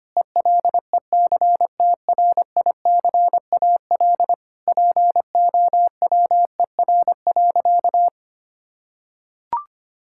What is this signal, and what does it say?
Morse code, heard as a keyed tone, about 25 wpm, 700 Hz. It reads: ELECTRICAL POWER.
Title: Sentences from Top 1000 Words - Review Part 6 - 25wpm